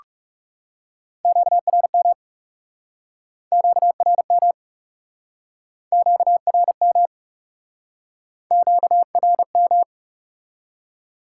Morse code, heard as a keyed tone, about 40 wpm, 700 Hz. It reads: QRM QRM QRM QRM